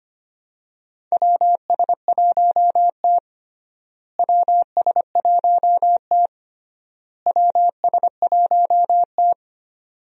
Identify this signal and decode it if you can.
Morse code, heard as a keyed tone, about 25 wpm, 700 Hz. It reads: WS1T WS1T WS1T
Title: USA Call Signs - 4 Characters - Repeated 3x - 25wpm